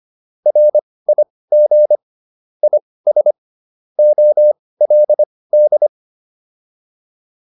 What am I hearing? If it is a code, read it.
Morse code, heard as a keyed tone, about 25 wpm, spaced as for 18 wpm, 600 Hz. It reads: RIG IS OLD